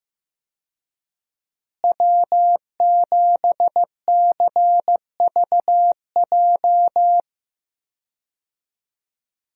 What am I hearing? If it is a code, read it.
Morse code, heard as a keyed tone, about 15 wpm, 700 Hz. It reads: W7CVJ
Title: USA Call Signs - 5 Characters - Repeated 2x - 15wpm